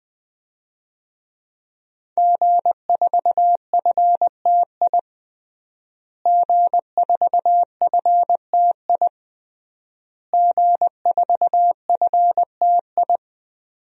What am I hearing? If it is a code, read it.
Morse code, heard as a keyed tone, about 21 wpm, 700 Hz. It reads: G4FTI G4FTI G4FTI
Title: DXCC-100 Call Signs - 5 Characters - Repeated 3x - 20wpm